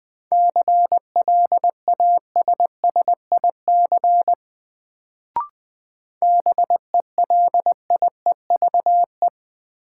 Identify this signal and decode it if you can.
Morse code, heard as a keyed tone, about 20 wpm, 700 Hz. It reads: CLASSIC BELIEVE
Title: Sentences from Top 1000 Words - New Word Drill - 20wpm